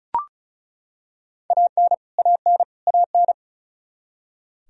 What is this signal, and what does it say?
Morse code, heard as a keyed tone, about 36 wpm, 700 Hz. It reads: AN AN AN
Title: Top 100 Words Repeated 35wpm